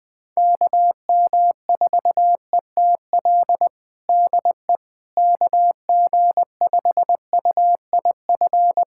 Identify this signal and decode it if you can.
Morse code, heard as a keyed tone, about 21 wpm, 700 Hz. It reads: KM4ETL DE KG5UIF
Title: USA Callsign DE Callsign - 6 Characters - 20wpm